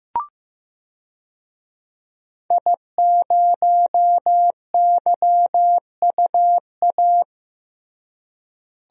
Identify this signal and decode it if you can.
Morse code, heard as a keyed tone, about 15 wpm, 700 Hz. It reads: I0YUA
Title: DXCC-100 Call Signs - 5 Characters - Repeated 3x - 15wpm